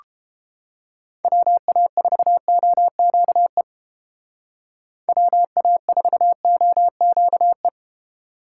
Morse code, heard as a keyed tone, about 30 wpm, 700 Hz. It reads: WA4OQE WA4OQE